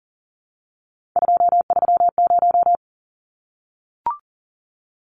Morse code, heard as a keyed tone, about 40 wpm, 700 Hz. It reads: 230
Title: Numbers - 3-Digit 40wpm